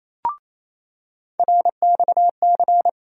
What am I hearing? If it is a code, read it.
Morse code, heard as a keyed tone, about 28 wpm, 700 Hz. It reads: RXC
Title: USA Call Sign Postfixes - 3 Letters 28wpm